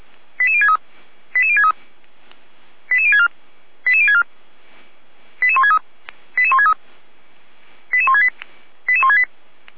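Five-tone selective call (ZVEI), heard as quick runs of five tones, tones of about 80 ms. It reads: 88063 88063 88064 88064 88163 88163 88168 88168
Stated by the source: ZVEI five-tone selective call received by SDR, code 88303.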